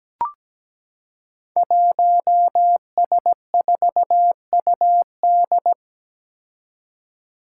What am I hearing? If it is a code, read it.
Morse code, heard as a keyed tone, about 17 wpm, 700 Hz. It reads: 1S4UD
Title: Five Challenging Letter-Number Combinations 17wpm